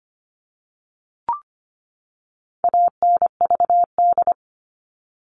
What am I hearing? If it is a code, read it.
Morse code, heard as a keyed tone, about 25 wpm, 700 Hz. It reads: ANVD
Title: All Cut Numbers - 4-Digit 25wpm